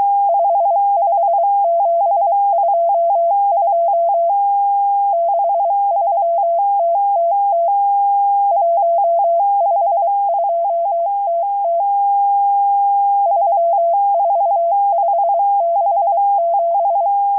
Morse code, heard as a keyed tone, about 23 wpm, spaced as for 21 wpm, 680 Hz. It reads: 55722 63TTT 152TT 34567
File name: HM02- -0437utc-7351khz